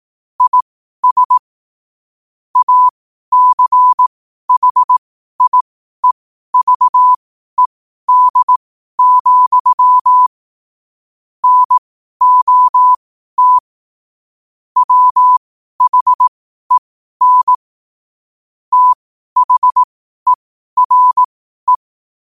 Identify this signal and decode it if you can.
Morse code, heard as a keyed tone, about 18 wpm, spaced as for 12 wpm, 1000 Hz. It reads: IS ACHIEVED, NOT WHEN THERE
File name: qotdfiles / 12wpm